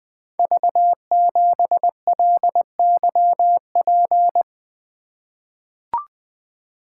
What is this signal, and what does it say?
Morse code, heard as a keyed tone, about 20 wpm, 700 Hz. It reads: V7LYP
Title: Code Groups - 5 Letter:Numbers - 20wpm